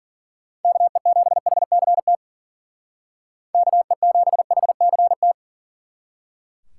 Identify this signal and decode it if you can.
Morse code, heard as a keyed tone, about 48 wpm, 700 Hz. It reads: KE7HCT KE7HCT
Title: Speed Racing - USA 2x3 Call Signs 40wpm